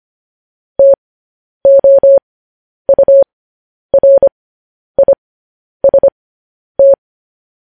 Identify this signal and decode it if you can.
Morse code, heard as a keyed tone, about 25 wpm, spaced as for 10 wpm, 550 Hz. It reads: TOURIST
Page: news-headlines-in-morse-code-at-10-wpm